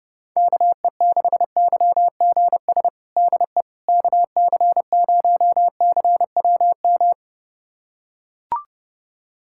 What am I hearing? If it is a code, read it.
Morse code, heard as a keyed tone, about 30 wpm, 700 Hz. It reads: KE6YGS DE KC0CWM
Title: USA Callsign DE Callsign 30wpm